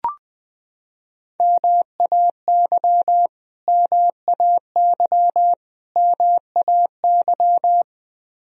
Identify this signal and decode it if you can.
Morse code, heard as a keyed tone, about 21 wpm, 700 Hz. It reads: MAY MAY MAY
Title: Top 100 Words Repeated 20wpm